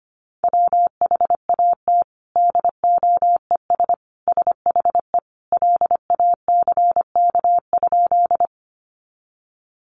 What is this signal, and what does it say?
Morse code, heard as a keyed tone, about 25 wpm, 700 Hz. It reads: WHAT DOES SHE LACK?
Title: Sentences from Top 1100 Words 25wpm